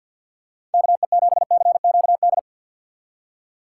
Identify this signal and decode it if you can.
Morse code, heard as a keyed tone, about 47 wpm, 700 Hz. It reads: KE7CQD
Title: USA Call Signs - 6 Characters - 3x Character Spacing - 50wpm